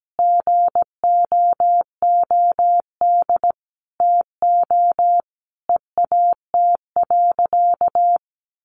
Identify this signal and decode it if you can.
Morse code, heard as a keyed tone, about 17 wpm, 700 Hz. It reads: GOOD TO EAT.
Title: Sentences from Top 800 Words 17wpm